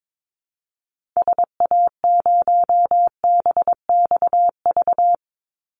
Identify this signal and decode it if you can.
Morse code, heard as a keyed tone, about 22 wpm, 700 Hz. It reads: SA0BXV
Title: Call Signs Repeated 3x - 22wpm